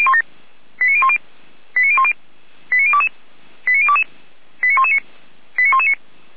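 Five-tone selective call (ZVEI), heard as quick runs of five tones, tones of about 70 ms. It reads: 89010 89010 89022 89022 89119 89119
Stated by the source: ZVEI five-tone selective call received on SDR.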